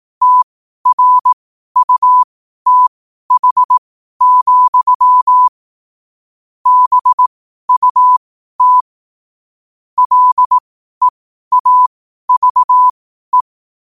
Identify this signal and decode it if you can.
Morse code, heard as a keyed tone, about 18 wpm, spaced as for 12 wpm, 1000 Hz. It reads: TRUTH, BUT LEAVE
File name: qotdfiles / 12wpm